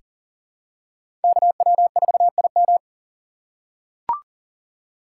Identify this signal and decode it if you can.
Morse code, heard as a keyed tone, about 40 wpm, 700 Hz. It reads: KW4IM
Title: USA Call Signs - 5 Characters - 3x Character Spacing - 40wpm